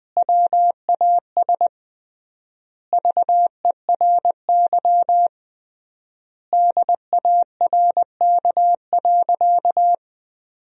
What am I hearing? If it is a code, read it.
Morse code, heard as a keyed tone, about 20 wpm, 700 Hz. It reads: WAS VERY DARK.
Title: 3x Word-Spacing - Sentences from Top 500 Words - 20wpm